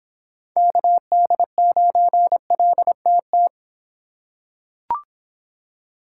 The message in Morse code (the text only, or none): KD9LTT